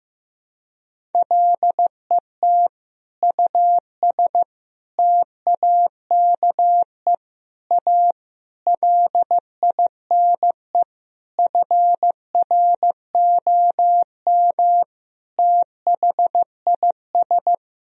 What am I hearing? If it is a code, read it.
Morse code, heard as a keyed tone, about 15 wpm, 700 Hz. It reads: LET US TAKE A LINE FROM THIS
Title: Sentences from Top 1000 Words - Review Part 1 - 15wpm